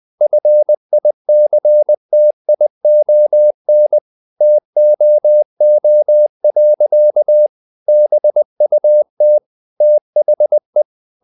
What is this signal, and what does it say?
Morse code, heard as a keyed tone, about 20 wpm, 600 Hz. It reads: FICTION TOO. BUT THE
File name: the old man and the sea / the old man and the sea part 1 20wpm